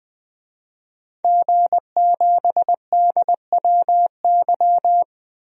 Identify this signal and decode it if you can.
Morse code, heard as a keyed tone, about 20 wpm, 700 Hz. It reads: G7DWY
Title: DXCC-100 Call Signs - 5 Characters - 20wpm